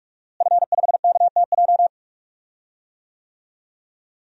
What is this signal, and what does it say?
Morse code, heard as a keyed tone, about 46 wpm, 700 Hz. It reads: F5KTJ
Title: DXCC-100 Call Signs - 5 Characters - Repeated 3x - 45wpm